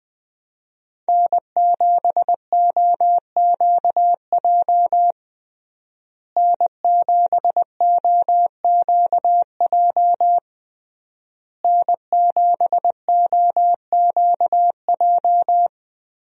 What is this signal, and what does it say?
Morse code, heard as a keyed tone, about 20 wpm, 700 Hz. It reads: N7OQJ N7OQJ N7OQJ
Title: USA Call Signs - 5 Characters - Repeated 3x - 20wpm